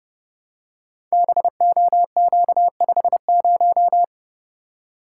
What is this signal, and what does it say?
Morse code, heard as a keyed tone, about 30 wpm, 700 Hz. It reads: BOQ50
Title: Code Groups - 5 Letter:Numbers - 30wpm